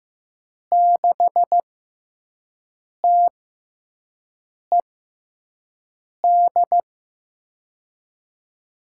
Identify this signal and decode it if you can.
Morse code, heard as a keyed tone, about 15 wpm, spaced as for 5 wpm, 700 Hz. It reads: 6TED
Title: USA Call Signs - 6 Characters - 6x Character Spacing - 15wpm